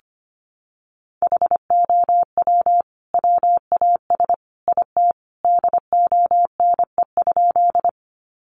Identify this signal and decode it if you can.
Morse code, heard as a keyed tone, about 25 wpm, 700 Hz. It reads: HOW WAS IT DONE?
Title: Sentences from Top 500 Words 25wpm